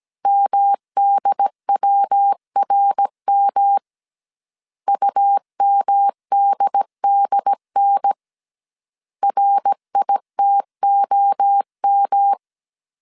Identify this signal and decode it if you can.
Morse code, heard as a keyed tone, about 17 wpm, 800 Hz. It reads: MDWRM UMDDN RITOM